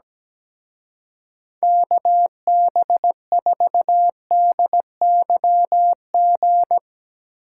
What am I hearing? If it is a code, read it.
Morse code, heard as a keyed tone, about 19 wpm, 700 Hz. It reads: KB4DYG
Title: Speed Racing - USA 2x3 Call Signs 17wpm